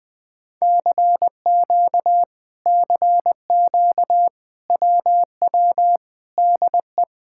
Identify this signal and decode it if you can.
Morse code, heard as a keyed tone, about 20 wpm, 700 Hz. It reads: CQ CQ WW DE
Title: Speed-Racing Common QSO Phrases - 20wpm